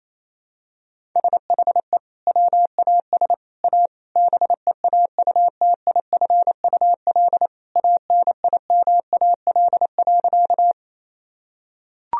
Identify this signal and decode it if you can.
Morse code, heard as a keyed tone, about 28 wpm, 700 Hz. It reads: SHE WAS A BEAUTIFUL ANIMAL.
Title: Sentences from Top 1000 Words - Review Part 1 - 28wpm